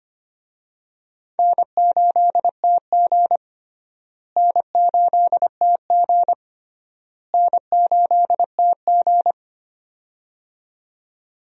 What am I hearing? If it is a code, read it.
Morse code, heard as a keyed tone, about 25 wpm, 700 Hz. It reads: N8TG N8TG N8TG
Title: USA Call Signs - 4 Characters - Repeated 3x - 25wpm